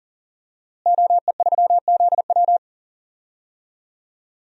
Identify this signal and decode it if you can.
Morse code, heard as a keyed tone, about 40 wpm, 700 Hz. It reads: OE3ZW
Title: DXCC-100 Call Signs - 5 Characters - Repeated 3x - 40wpm